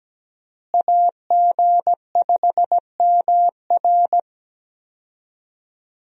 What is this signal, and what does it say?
Morse code, heard as a keyed tone, about 17 wpm, 700 Hz. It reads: AG5MR
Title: USA Call Signs - 5 Characters - Repeated 3x - 17wpm